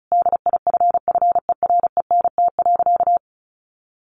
Morse code, heard as a keyed tone, about 34 wpm, 700 Hz. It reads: DIFFERENT.
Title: Sentences from Top 1500 Words 35wpm